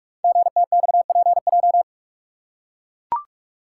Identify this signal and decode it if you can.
Morse code, heard as a keyed tone, about 44 wpm, 700 Hz. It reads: GTXPJ
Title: Code Groups - 5 Letters - 45wpm